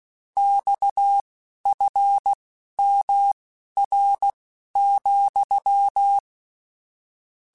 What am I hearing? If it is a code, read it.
Morse code, heard as a keyed tone, about 16 wpm, spaced as for 11 wpm, 800 Hz. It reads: XFMR,
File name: Technical Subjects / Technical Subjects-mp3s / Sw Pwr Sup @10